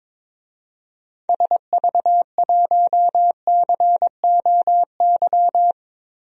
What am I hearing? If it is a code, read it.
Morse code, heard as a keyed tone, about 22 wpm, 700 Hz. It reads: SV1COY
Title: DXCC-100 Call Signs - 6 Characters - 22wpm